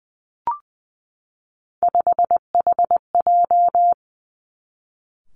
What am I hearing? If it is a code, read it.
Morse code, heard as a keyed tone, about 20 wpm, 700 Hz. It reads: 5HJ
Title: Three Challenging Letter-Number Combinations 20wpm